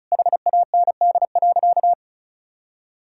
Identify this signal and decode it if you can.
Morse code, heard as a keyed tone, about 36 wpm, 700 Hz. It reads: HAND.
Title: Sentences from Top 1000 Words - Review Part 5 - 35wpm